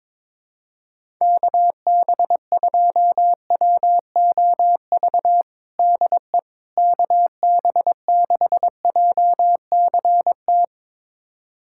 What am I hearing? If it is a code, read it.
Morse code, heard as a keyed tone, about 22 wpm, 700 Hz. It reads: KB2WOV DE KB6JCT
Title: USA Callsign DE Callsign - 6 Characters - 22wpm